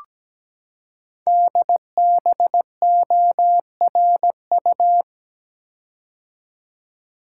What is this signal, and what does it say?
Morse code, heard as a keyed tone, about 17 wpm, 700 Hz. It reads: DBORU
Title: Code Groups - 5 Letters - 17wpm